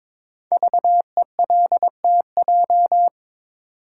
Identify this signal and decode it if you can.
Morse code, heard as a keyed tone, about 23 wpm, 700 Hz. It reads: VELTJ
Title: Code Groups - 5 Letter:Numbers - 22wpm